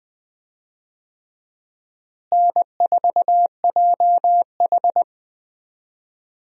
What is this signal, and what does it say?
Morse code, heard as a keyed tone, about 21 wpm, 700 Hz. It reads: N4JH
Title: USA Call Signs - 4 Characters - Repeated 3x - 20wpm